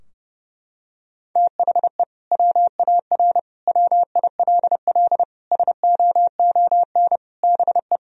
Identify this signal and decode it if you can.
Morse code, heard as a keyed tone, about 32 wpm, 700 Hz. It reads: THE WAR WILL SOON BE
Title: Sentences from Top 1000 Words - Review Part 2 - 30wpm